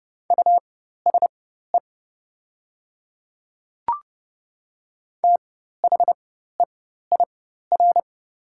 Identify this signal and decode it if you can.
Morse code, heard as a keyed tone, about 30 wpm, spaced as for 14 wpm, 700 Hz. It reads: USE THEIR
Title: 4x Character Spacing - Top 100 Words - 30wpm